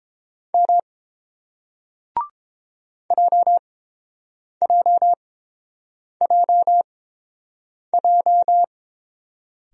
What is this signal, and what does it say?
Morse code, heard as a keyed tone, about 30 wpm, spaced as for 7 wpm, 700 Hz. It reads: M JJJJ